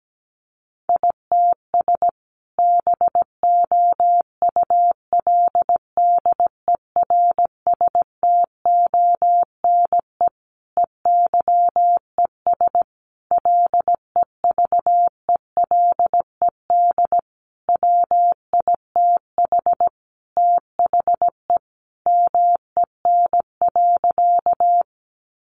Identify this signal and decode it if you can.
Morse code, heard as a keyed tone, about 17 wpm, 700 Hz. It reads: ITS BOULDERSTONE EYES LEVELED WITH THE MEN.
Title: A Sound Of Thunder 17wpm